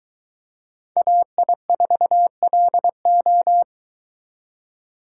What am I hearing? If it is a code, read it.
Morse code, heard as a keyed tone, about 22 wpm, 700 Hz. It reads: AI4LO